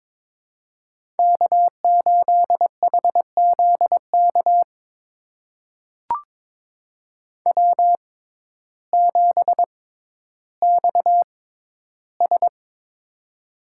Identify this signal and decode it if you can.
Morse code, heard as a keyed tone, about 22 wpm, spaced as for 12 wpm, 700 Hz. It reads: K8HZK W7XS
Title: USA Call Signs - 6x Character Spacing - 22wpm